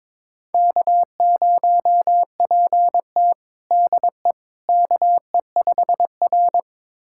Morse code, heard as a keyed tone, about 21 wpm, 700 Hz. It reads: K0PT DE KE5R
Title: USA Callsign DE Callsign - 4 Characters - 22wpm